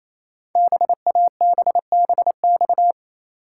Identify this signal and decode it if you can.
Morse code, heard as a keyed tone, about 27 wpm, 700 Hz. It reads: BABBX
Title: Code Groups - 5 Letters - 28wpm